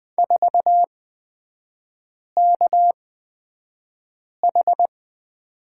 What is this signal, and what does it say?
Morse code, heard as a keyed tone, about 20 wpm, spaced as for 5 wpm, 700 Hz. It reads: 4KH
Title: 224 Lesson 36e - BT - Three Characters 20wpm